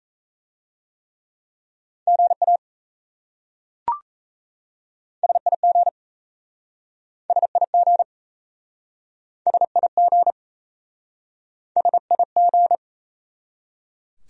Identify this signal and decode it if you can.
Morse code, heard as a keyed tone, about 37 wpm, 700 Hz. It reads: GA SIG SIG SIG SIG